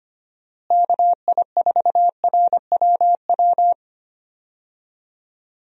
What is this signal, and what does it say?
Morse code, heard as a keyed tone, about 25 wpm, 700 Hz. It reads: KI4RWW
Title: USA Call Signs - 6 Characters - Repeated 2x - 25wpm